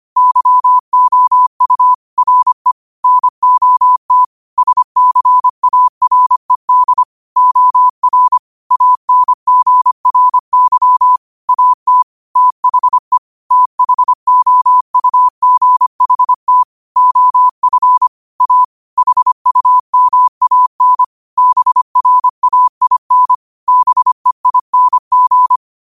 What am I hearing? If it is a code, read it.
Morse code, heard as a keyed tone, about 25 wpm, 1000 Hz. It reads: YOU RE NOT SCARED OR ANGRY AT THE THOUGHT OF A HUMAN BRAIN BEING
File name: qotdfiles / 25wpm